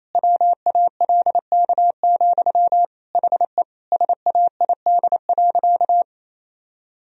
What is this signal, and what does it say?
Morse code, heard as a keyed tone, about 28 wpm, 700 Hz. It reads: WALK, HE SAID.
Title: Sentences from Top 1000 Words - Review Part 1 - 28wpm